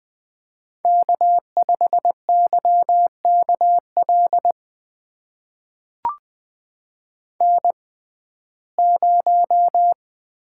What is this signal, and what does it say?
Morse code, heard as a keyed tone, about 21 wpm, spaced as for 15 wpm, 700 Hz. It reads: K5YKL N0
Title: USA Call Signs - 5 Characters - 6x Character Spacing - 20wpm